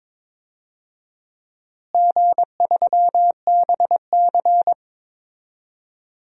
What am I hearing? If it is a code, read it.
Morse code, heard as a keyed tone, about 22 wpm, 700 Hz. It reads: G3BC
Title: DXCC-100 Call Signs - 4 Characters - Repeated 3x - 22wpm